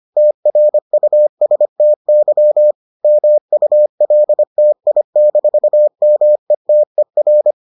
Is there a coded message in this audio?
Yes: Morse code from 0.2 to 7.5 s, heard as a keyed tone, about 25 wpm, 600 Hz. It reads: TRUSTY MULTI-METER